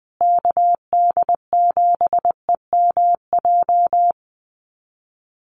Morse code, heard as a keyed tone, about 20 wpm, 700 Hz. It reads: KD7EMJ
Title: USA Call Signs Repeated 3x 20wpm